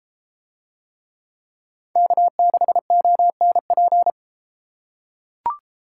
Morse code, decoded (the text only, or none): K6ONP